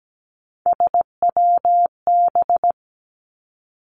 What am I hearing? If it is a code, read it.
Morse code, heard as a keyed tone, about 17 wpm, 700 Hz. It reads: SWB
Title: Three Challenging Letter-Number Combinations 17wpm